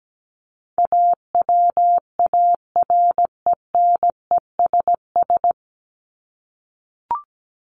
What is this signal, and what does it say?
Morse code, heard as a keyed tone, about 17 wpm, 700 Hz. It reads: AWARENESS
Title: Top 1501-1750 Words 17wpm